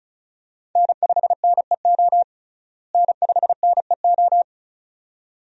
Morse code, heard as a keyed tone, about 34 wpm, 700 Hz. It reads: N5NEO N5NEO
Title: USA Call Signs - 5 Characters - Repeated 2x - 35wpm